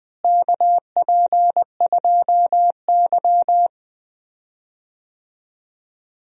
Morse code, heard as a keyed tone, about 20 wpm, 700 Hz. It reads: KP2Y
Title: DXCC-100 Call Signs - 4 Characters - Repeated 2x - 20wpm